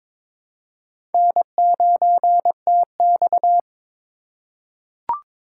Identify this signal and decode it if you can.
Morse code, heard as a keyed tone, about 22 wpm, 700 Hz. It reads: N9TX
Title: USA Call Signs - 4 Characters - 4x Character Spacing - 22wpm